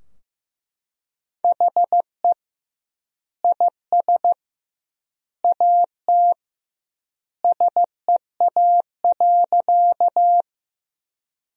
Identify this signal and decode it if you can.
Morse code, heard as a keyed tone, about 15 wpm, 700 Hz. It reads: HE IS AT SEA.
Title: 2x Word-Spacing - Sentences from Top 300 Words - 15wpm